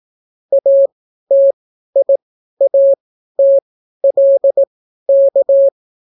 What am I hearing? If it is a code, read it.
Morse code, heard as a keyed tone, about 18 wpm, spaced as for 12 wpm, 550 Hz. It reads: ATIATLK